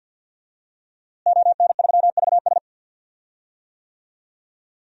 Morse code, heard as a keyed tone, about 48 wpm, 700 Hz. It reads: ON3VS
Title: DXCC-100 Call Signs - 5 Characters - Repeated 3x - 50wpm